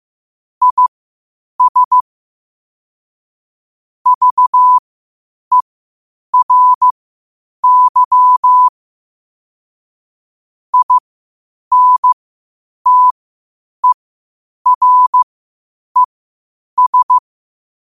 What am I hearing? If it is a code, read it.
Morse code, heard as a keyed tone, about 15 wpm, spaced as for 8 wpm, 1000 Hz. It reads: IS VERY INTERES
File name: qotdfiles / 8wpm